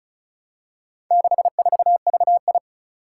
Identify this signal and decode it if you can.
Morse code, heard as a keyed tone, about 36 wpm, 700 Hz. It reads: 64VI
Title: Four Challenging Letter-Number Combinations 35wpm